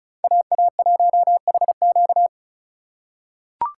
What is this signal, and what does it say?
Morse code, heard as a keyed tone, about 34 wpm, 700 Hz. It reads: AA1HQ
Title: USA Call Signs - 5 Characters - 2x Character Spacing - 35wpm